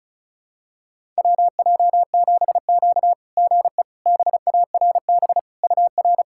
Morse code, heard as a keyed tone, about 36 wpm, 690 Hz. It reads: WJ7Q GE BARB UR
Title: SOTA QSO Practice 35wpm